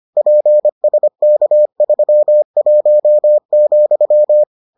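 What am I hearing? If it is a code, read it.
Morse code, heard as a keyed tone, about 25 wpm, 600 Hz. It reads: PSK31,
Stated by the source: Morse edition of the Foundations of Amateur Radio episodes.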